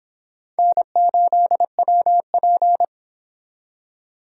Morse code, heard as a keyed tone, about 25 wpm, 700 Hz. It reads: N8WP